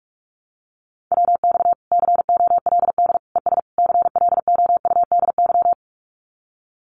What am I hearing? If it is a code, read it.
Morse code, heard as a keyed tone, about 44 wpm, 700 Hz. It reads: WX COLD ES CLOUDY
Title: Common QSO Phrases Encore 45wpm